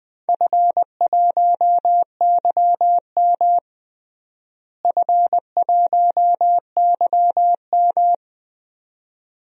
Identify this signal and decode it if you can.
Morse code, heard as a keyed tone, about 20 wpm, 700 Hz. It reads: F1YM F1YM
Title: DXCC-100 Call Signs - 4 Characters - Repeated 2x - 20wpm